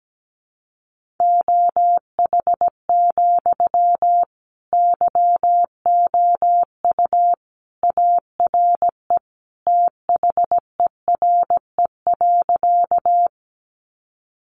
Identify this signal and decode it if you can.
Morse code, heard as a keyed tone, about 17 wpm, 700 Hz. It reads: OH, YOU ARE THERE.
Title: Sentences from Top 500 Words 17wpm